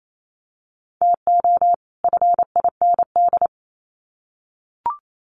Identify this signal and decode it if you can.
Morse code, heard as a keyed tone, about 28 wpm, 700 Hz. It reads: TO FIND
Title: Sets of 2 Words 28wpm